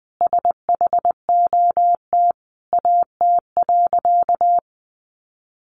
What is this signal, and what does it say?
Morse code, heard as a keyed tone, about 20 wpm, 700 Hz. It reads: SHOT AT.
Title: Sentences from Top 1100 Words 20wpm